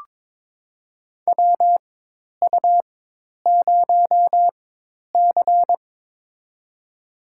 Morse code, heard as a keyed tone, about 21 wpm, spaced as for 10 wpm, 700 Hz. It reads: WU0C
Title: USA Call Signs - 4 Characters - 4x Character Spacing - 22wpm